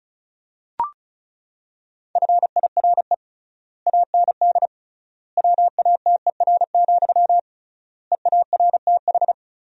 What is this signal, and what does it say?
Morse code, heard as a keyed tone, about 36 wpm, 700 Hz. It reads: FIRE AND WATER, EARTH
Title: 3x Word-Spacing - Sentences from Top 400 Words - 35wpm